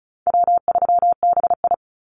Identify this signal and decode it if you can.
Morse code, heard as a keyed tone, about 34 wpm, 700 Hz. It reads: W3BI
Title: Easy Call Signs Repeated 2x - 35wpm